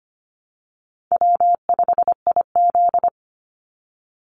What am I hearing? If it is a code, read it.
Morse code, heard as a keyed tone, about 25 wpm, 700 Hz. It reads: W5IZ
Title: Easy Call Signs - 25wpm